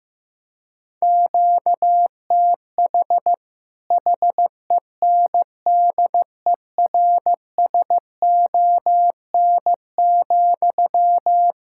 Morse code, heard as a keyed tone, about 15 wpm, 700 Hz. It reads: QTH HENDERSON,
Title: Practice QSO - Exchange 1 - Beginning - 15wpm